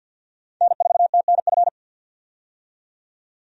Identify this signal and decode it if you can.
Morse code, heard as a keyed tone, about 51 wpm, 700 Hz. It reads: N4TNF
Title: USA Call Signs - 5 Characters - 50wpm